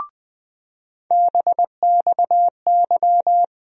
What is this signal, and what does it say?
Morse code, heard as a keyed tone, about 20 wpm, 700 Hz. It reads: BXY
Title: USA Call Sign Postfixes - 3 Letters 20wpm